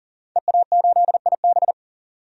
Morse code, heard as a keyed tone, about 40 wpm, 700 Hz. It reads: EA8IB
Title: DXCC-100 Call Signs - 5 Characters - 40wpm